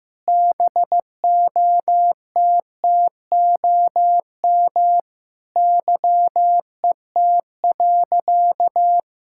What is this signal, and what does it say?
Morse code, heard as a keyed tone, about 15 wpm, 700 Hz. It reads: BOTTOM YET.
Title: Sentences from Top 1000 Words - Review Part 6 - 15wpm